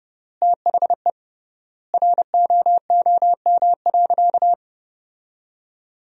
Morse code, heard as a keyed tone, about 28 wpm, 700 Hz. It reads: THE ROOM.
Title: 3x Word-Spacing - Sentences from Top 500 Words - 30wpm